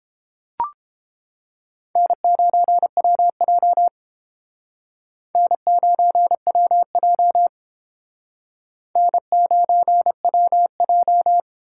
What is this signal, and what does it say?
Morse code, heard as a keyed tone, about 30 wpm, 700 Hz. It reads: N9WJ N9WJ N9WJ